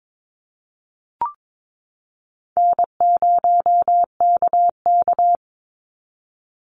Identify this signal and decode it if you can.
Morse code, heard as a keyed tone, about 22 wpm, 700 Hz. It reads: N0KK
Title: Easy Call Signs - 22wpm